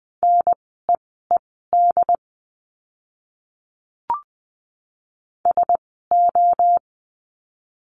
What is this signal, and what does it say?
Morse code, heard as a keyed tone, about 20 wpm, spaced as for 15 wpm, 700 Hz. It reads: NEED SO
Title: 230 Lesson 37f - Letter x - Words 20wpm